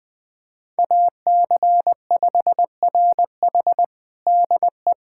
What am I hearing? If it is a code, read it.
Morse code, heard as a keyed tone, about 21 wpm, 700 Hz. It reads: AC5RH DE
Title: USA Callsign DE Callsign 20wpm